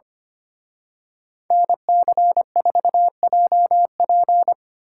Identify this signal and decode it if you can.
Morse code, heard as a keyed tone, about 30 wpm, 700 Hz. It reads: NC4JP
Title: Speed Racing - USA 2x2 Call Signs 25wpm